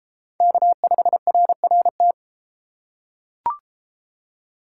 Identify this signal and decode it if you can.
Morse code, heard as a keyed tone, about 30 wpm, 700 Hz. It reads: K5RRT